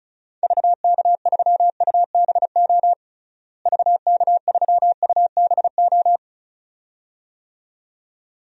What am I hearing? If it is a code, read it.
Morse code, heard as a keyed tone, about 35 wpm, 700 Hz. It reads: VK3UBO VK3UBO
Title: DXCC-100 Call Signs - 6 Characters - Repeated 2x - 35wpm